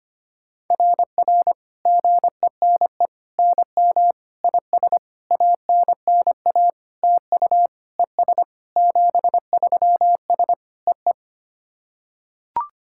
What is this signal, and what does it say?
Morse code, heard as a keyed tone, about 25 wpm, 690 Hz. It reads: RR GENE NM IS ANNA TU ES 73S EE
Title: SOTA QSO Practice 25wpm